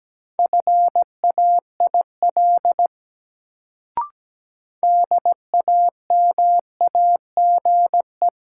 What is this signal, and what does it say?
Morse code, heard as a keyed tone, about 17 wpm, 700 Hz. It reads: FAIL DAMAGE
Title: Sentences from Top 1100 Words - New Word Drill Alt - 17wpm